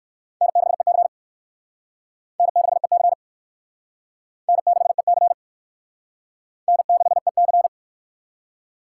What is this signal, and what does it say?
Morse code, heard as a keyed tone, about 58 wpm, 700 Hz. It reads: N6EC N6EC N6EC N6EC